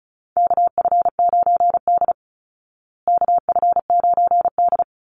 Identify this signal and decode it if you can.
Morse code, heard as a keyed tone, about 34 wpm, 700 Hz. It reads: KF9D KF9D
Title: Call Signs Repeated 2x - 35wpm